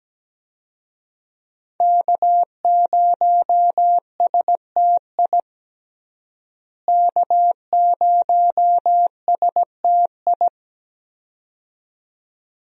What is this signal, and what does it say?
Morse code, heard as a keyed tone, about 17 wpm, 700 Hz. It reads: K0STI K0STI
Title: USA Call Signs - 5 Characters - Repeated 2x - 17wpm